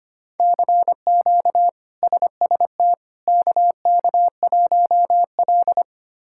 Morse code, heard as a keyed tone, about 25 wpm, 700 Hz. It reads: CQ SST KK1L
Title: K1USN SST QSO Practice - 25wpm